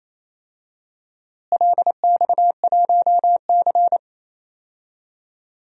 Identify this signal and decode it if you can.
Morse code, heard as a keyed tone, about 28 wpm, 700 Hz. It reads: LX1C
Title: DXCC-100 Call Signs - 4 Characters - Repeated 2x - 28wpm